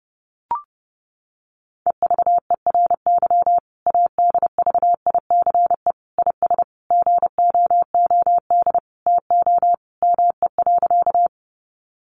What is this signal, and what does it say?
Morse code, heard as a keyed tone, about 30 wpm, 700 Hz. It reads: EVERY ADVICE IS GOOD TO ME.
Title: Sentences from Top 1100 Words 30wpm